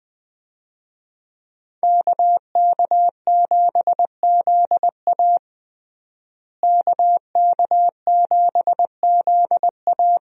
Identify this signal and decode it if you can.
Morse code, heard as a keyed tone, about 20 wpm, 700 Hz. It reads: KK7ZA KK7ZA
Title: USA Call Signs - 5 Characters - Repeated 3x - 20wpm